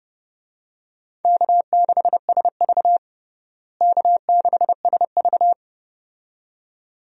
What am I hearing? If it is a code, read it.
Morse code, heard as a keyed tone, about 29 wpm, 700 Hz. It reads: K6SV K6SV
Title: USA Call Signs - 4 Characters - Repeated 2x - 30wpm